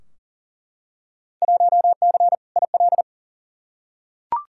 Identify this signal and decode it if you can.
Morse code, heard as a keyed tone, about 40 wpm, 700 Hz. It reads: 1C IL
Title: Field Day Exchanges - 40wpm